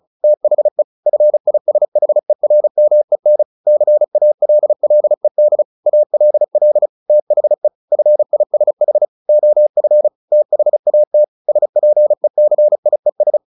Morse code, heard as a keyed tone, about 34 wpm, 600 Hz. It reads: THE FISHERMEN CALLED ALL THE FISH OF THAT SPECIES
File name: the old man and the sea / the old man and the sea 1 35wpm